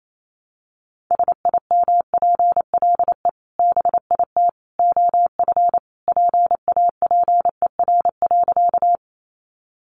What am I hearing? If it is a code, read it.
Morse code, heard as a keyed tone, about 28 wpm, 690 Hz. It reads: SIMPLE BIT OF PAPER.
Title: Sentences from Top 900 Words 28wpm